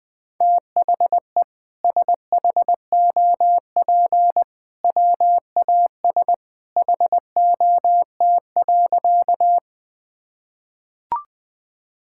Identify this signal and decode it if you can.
Morse code, heard as a keyed tone, about 21 wpm, 700 Hz. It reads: THE SHOP WAS HOT.
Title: Sentences from Top 1600 Words 20wpm